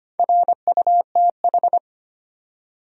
Morse code, heard as a keyed tone, about 25 wpm, 700 Hz. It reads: RUTH
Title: USA Common Names - Max 4 Letters - 25wpm